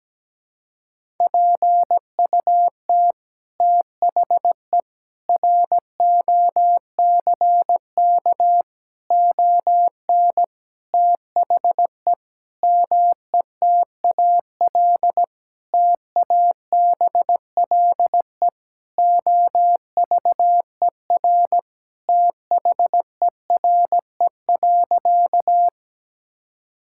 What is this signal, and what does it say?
Morse code, heard as a keyed tone, about 17 wpm, 700 Hz. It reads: PUT THE ROCK ON THE METAL TABLE OVER THERE.
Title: Sentences from Top 1000 Words - Review Part 6 - 17wpm